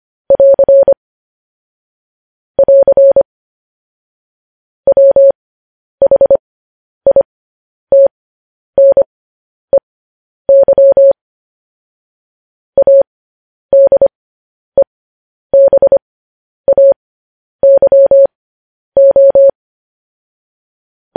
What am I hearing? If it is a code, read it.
Morse code, heard as a keyed tone, about 25 wpm, spaced as for 10 wpm, 550 Hz. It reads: <AR> <AR> WHITNEY ADEBAYO
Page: news-headlines-in-morse-code-at-10-wpm